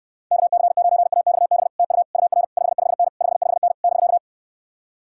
Morse code, heard as a keyed tone, about 71 wpm, 700 Hz. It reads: KC9AXB TU UR 55N 55N <BK>